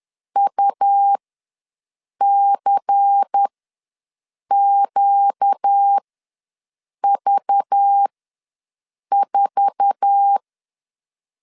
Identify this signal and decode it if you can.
Morse code, heard as a keyed tone, about 10 wpm, spaced as for 6 wpm, 800 Hz. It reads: UCQV4
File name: LEZIONI / 8 YJ45 / 8 YJ4 ws 10wpm 4020